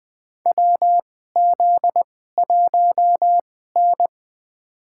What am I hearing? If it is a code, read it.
Morse code, heard as a keyed tone, about 20 wpm, spaced as for 14 wpm, 700 Hz. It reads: WZ1N
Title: USA Call Signs - 4 Characters - 2x Character Spacing - 20wpm